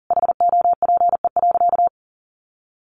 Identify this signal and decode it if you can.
Morse code, heard as a keyed tone, about 40 wpm, 700 Hz. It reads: HOPE.
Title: Sentences from Top 1000 Words 40wpm